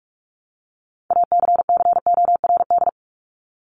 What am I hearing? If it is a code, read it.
Morse code, heard as a keyed tone, about 44 wpm, 700 Hz. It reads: ACCORD
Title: ING Suffix - Top 100 Words with Base Words 45wpm